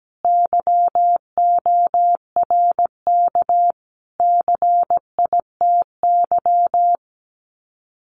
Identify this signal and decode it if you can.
Morse code, heard as a keyed tone, about 17 wpm, 700 Hz. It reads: YORK CITY
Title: Sets of 5 Words Encore2 17wpm